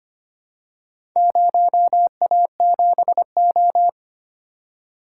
Morse code, heard as a keyed tone, about 25 wpm, 700 Hz. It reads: 0A7O
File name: practice / 25wpm / Four Letter-Number Combinations - 25wpm